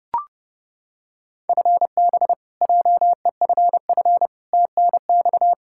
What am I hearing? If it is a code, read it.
Morse code, heard as a keyed tone, about 30 wpm, 700 Hz. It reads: FB JEFF TNX
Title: Practice QSO - Exchange 3 - More Info - 30wpm